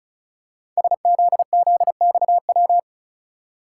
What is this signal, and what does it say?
Morse code, heard as a keyed tone, about 36 wpm, 700 Hz. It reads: SZZXW
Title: Code Groups - 5 Letters - 35wpm